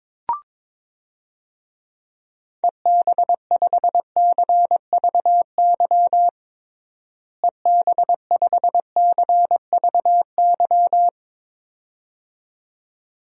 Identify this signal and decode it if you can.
Morse code, heard as a keyed tone, about 22 wpm, 700 Hz. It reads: EB5CVY EB5CVY
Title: DXCC-100 Call Signs - 6 Characters - Repeated 2x - 22wpm